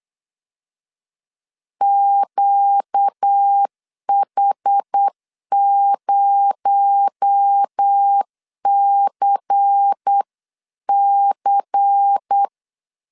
Morse code, heard as a keyed tone, about 8 wpm, 800 Hz. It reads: QH0CC
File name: CW / DET / 1 6 1500C S B 8wpm 5142